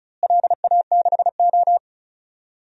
Morse code, heard as a keyed tone, about 36 wpm, 700 Hz. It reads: LA6O
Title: DXCC-100 Call Signs - 4 Characters - Repeated 3x - 35wpm